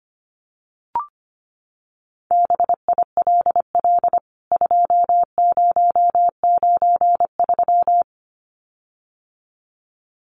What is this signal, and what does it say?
Morse code, heard as a keyed tone, about 25 wpm, 700 Hz. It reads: BILL 2093
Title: CWT Exchange 25wpm